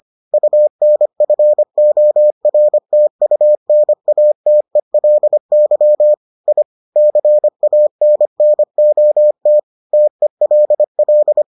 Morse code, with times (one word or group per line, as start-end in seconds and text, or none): 0.3-6.1 UNFORTUNATELY
6.5-6.6 I
7.0-9.6 CANNOT
9.9-11.4 TELL